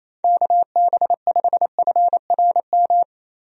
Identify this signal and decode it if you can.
Morse code, heard as a keyed tone, about 28 wpm, 700 Hz. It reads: KB5FRM
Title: USA Call Signs - 6 Characters - Repeated 2x - 28wpm